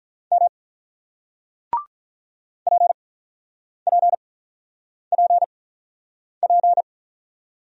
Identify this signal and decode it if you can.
Morse code, heard as a keyed tone, about 48 wpm, spaced as for 9 wpm, 700 Hz. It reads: M PPPP